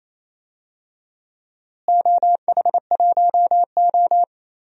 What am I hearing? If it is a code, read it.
Morse code, heard as a keyed tone, about 28 wpm, 700 Hz. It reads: OH1O
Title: DXCC-100 Call Signs - 4 Characters - Repeated 2x - 28wpm